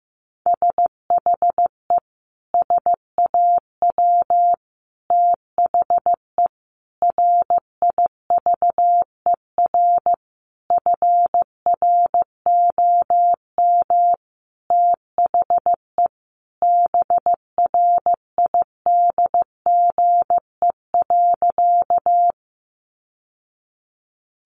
Morse code, heard as a keyed tone, about 15 wpm, 700 Hz. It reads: SHE SAW THE RIVER FROM THE BRIDGE.
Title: Sentences from Top 1200 Words 15wpm